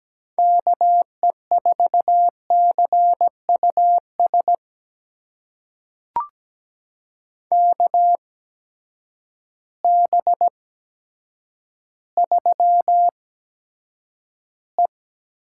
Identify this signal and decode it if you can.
Morse code, heard as a keyed tone, about 17 wpm, spaced as for 8 wpm, 700 Hz. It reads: KE4CUS KB3E
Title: USA Call Signs - 6 Characters - 8x Character Spacing - 17wpm